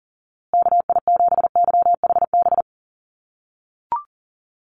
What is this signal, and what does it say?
Morse code, heard as a keyed tone, about 40 wpm, 700 Hz. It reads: KI7YHB
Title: USA Call Signs Repeated 3x 40wpm